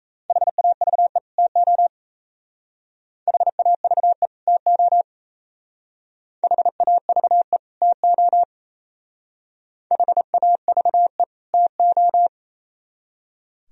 Morse code, heard as a keyed tone, about 36 wpm, 700 Hz. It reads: HAVE TO HAVE TO HAVE TO HAVE TO